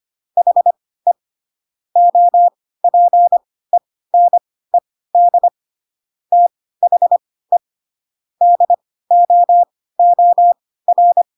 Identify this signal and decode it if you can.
Morse code, heard as a keyed tone, about 25 wpm, spaced as for 16 wpm, 700 Hz. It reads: HE OPENED THE DOOR
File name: Pinocchio / Pinocchio ch1 10 25 ik1hge